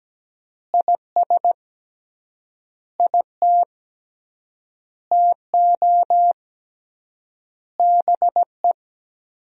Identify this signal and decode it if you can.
Morse code, heard as a keyed tone, about 17 wpm, 700 Hz. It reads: IS IT TO BE
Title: 3x Word-Spacing - Sentences from Top 300 Words - 17wpm